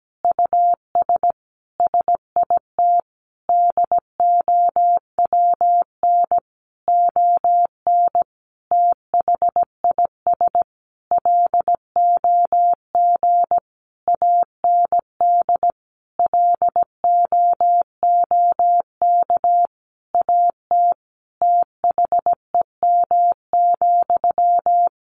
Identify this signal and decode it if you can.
Morse code, heard as a keyed tone, about 17 wpm, 700 Hz. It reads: US SIT DOWN ON THIS LOG AND LOOK AT THEM,
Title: Sentences from Top 1400 Words 17wpm